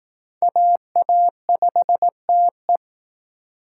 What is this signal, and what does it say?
Morse code, heard as a keyed tone, about 20 wpm, 700 Hz. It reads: AA5TE